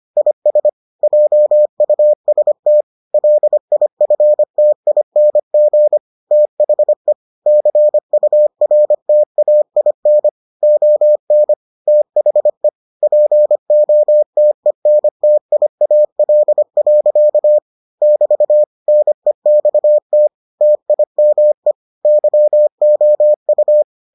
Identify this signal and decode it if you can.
Morse code, heard as a keyed tone, about 25 wpm, 600 Hz. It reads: IS JUST LIFTING THE CURTAIN ON THE POTENTIAL. <BT> NEXT TIME YOU